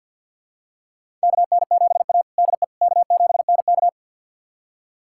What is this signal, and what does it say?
Morse code, heard as a keyed tone, about 46 wpm, 700 Hz. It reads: KN7A DE K7NK
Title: USA Callsign DE Callsign - 4 Characters - 50wpm